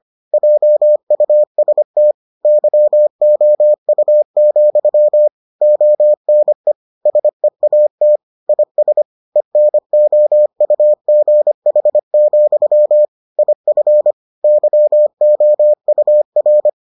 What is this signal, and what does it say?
Morse code, heard as a keyed tone, about 25 wpm, 600 Hz. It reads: JUST YOU, ONE SEAT IS ENOUGH, IF YOUR